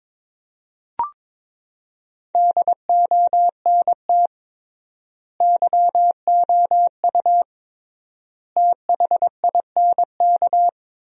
Morse code, heard as a keyed tone, about 23 wpm, 700 Hz. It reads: DONT YOU THINK
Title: 3x Word-Spacing - Sentences from Top 300 Words - 22wpm